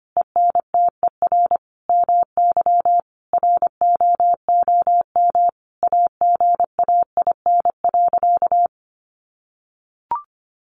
Morse code, heard as a keyed tone, about 25 wpm, 700 Hz. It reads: ENTER MY ROOM AGAIN.
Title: Sentences from Top 1300 Words 25wpm